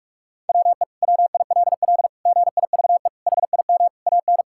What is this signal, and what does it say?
Morse code, heard as a keyed tone, about 41 wpm, 700 Hz. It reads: WE WILL GIVE HIM AN
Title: Sentences from Top 1000 Words - Review Part 5 - 45wpm